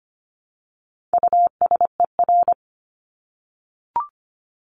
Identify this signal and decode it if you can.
Morse code, heard as a keyed tone, about 25 wpm, 700 Hz. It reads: USER
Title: Top 751-1000 Words 25wpm